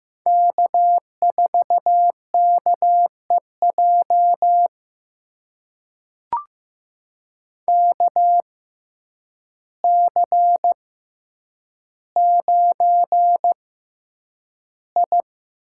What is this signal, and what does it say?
Morse code, heard as a keyed tone, about 15 wpm, spaced as for 8 wpm, 700 Hz. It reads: K4KEJ KC9I
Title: USA Call Signs - 6x Character Spacing - 15wpm